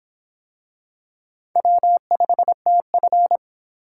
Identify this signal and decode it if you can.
Morse code, heard as a keyed tone, about 26 wpm, 700 Hz. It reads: W5TF